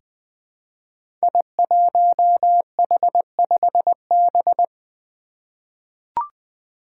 Morse code, heard as a keyed tone, about 20 wpm, 700 Hz. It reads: I1H5B
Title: Five Challenging Letter-Number Combinations 20wpm